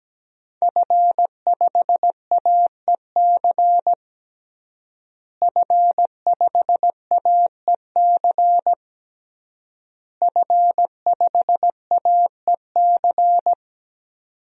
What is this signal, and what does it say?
Morse code, heard as a keyed tone, about 17 wpm, 700 Hz. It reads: F5AEC F5AEC F5AEC
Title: DXCC-100 Call Signs - 5 Characters - Repeated 3x - 17wpm